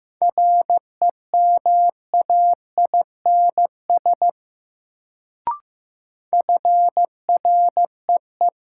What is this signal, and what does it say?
Morse code, heard as a keyed tone, about 15 wpm, 700 Hz. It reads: REMAINS FREE
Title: Sets of 5 Words Encore - New Word Drill - 15wpm